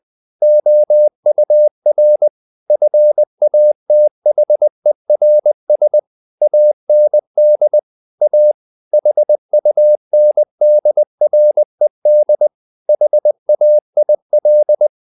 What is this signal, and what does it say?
Morse code, heard as a keyed tone, about 20 wpm, 600 Hz. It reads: OUR FATHERS AND A HUNDRED HAIL
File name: the old man and the sea part 2 20wpm